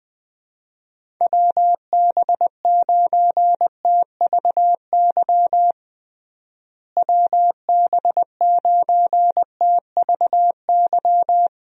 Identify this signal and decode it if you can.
Morse code, heard as a keyed tone, about 20 wpm, 700 Hz. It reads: WB9TVY WB9TVY
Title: USA Call Signs - 6 Characters - Repeated 2x - 20wpm